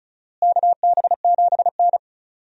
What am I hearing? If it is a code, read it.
Morse code, heard as a keyed tone, about 36 wpm, 700 Hz. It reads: KB7N
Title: USA Call Signs - 4 Characters - Repeated 3x - 35wpm